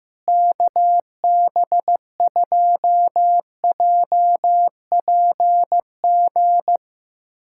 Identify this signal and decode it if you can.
Morse code, heard as a keyed tone, about 15 wpm, 700 Hz. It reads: KB2JPG
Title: USA Callsign DE Callsign - 6 Characters - 15wpm